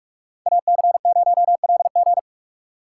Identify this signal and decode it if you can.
Morse code, heard as a keyed tone, about 41 wpm, 700 Hz. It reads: AC0LG